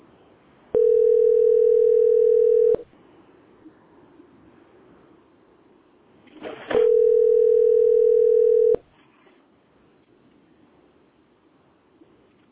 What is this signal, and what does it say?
Ringback tone (North America), 440 + 480 Hz.